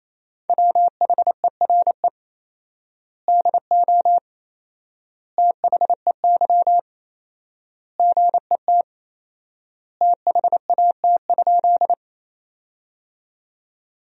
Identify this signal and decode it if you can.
Morse code, heard as a keyed tone, about 28 wpm, 700 Hz. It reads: WHERE DO THEY GET THAT?
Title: 4x Word-Spacing - Sentences from Top 200 Words - 28wpm